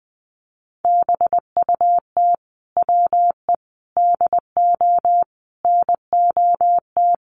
Morse code, heard as a keyed tone, about 20 wpm, 700 Hz. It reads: BUT WE DO NOT
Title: Sets of 5 Words Encore2 20wpm